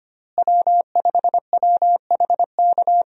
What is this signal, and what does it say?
Morse code, heard as a keyed tone, about 25 wpm, 700 Hz. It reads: W5WHK
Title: USA Call Signs - 5 Characters - Repeated 3x - 25wpm